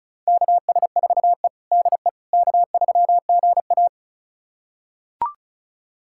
Morse code, heard as a keyed tone, about 34 wpm, 700 Hz. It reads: KS4E DE K3GA
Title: USA Callsign DE Callsign - 4 Characters - 35wpm